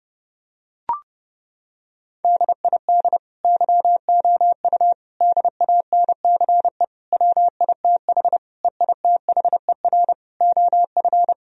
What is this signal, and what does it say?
Morse code, heard as a keyed tone, about 30 wpm, 700 Hz. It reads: DID YOU DANCE WITH EITHER OF
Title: Sentences from Top 1000 Words - Review Part 4 - 30wpm